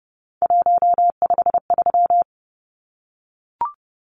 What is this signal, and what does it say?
Morse code, heard as a keyed tone, about 30 wpm, 700 Hz. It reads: 153
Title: Numbers - 3-Digit 30wpm